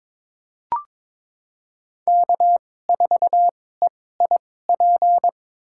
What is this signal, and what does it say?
Morse code, heard as a keyed tone, about 22 wpm, spaced as for 16 wpm, 700 Hz. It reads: K4EIP
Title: USA Call Signs - 2x Character Spacing - 22wpm